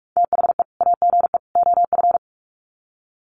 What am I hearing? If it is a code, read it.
Morse code, heard as a keyed tone, about 44 wpm, 700 Hz. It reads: THE AGE OF
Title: Sets of 3 Words 45wpm